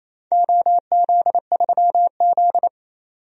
Morse code, heard as a keyed tone, about 28 wpm, 700 Hz. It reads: OZ3Z
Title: DXCC-100 Call Signs - 4 Characters - 28wpm